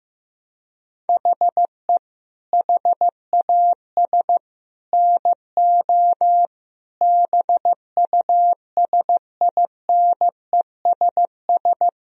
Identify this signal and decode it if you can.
Morse code, heard as a keyed tone, about 16 wpm, 700 Hz. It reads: HE HAS NO BUSINESS
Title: Sentences from Top 1000 Words - Review Part 4 - 15wpm